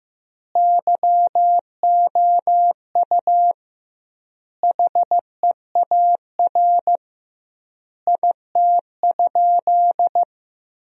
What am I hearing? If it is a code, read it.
Morse code, heard as a keyed tone, about 15 wpm, 700 Hz. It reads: YOU HEAR IT?
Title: 2x Word-Spacing - Sentences from Top 300 Words - 15wpm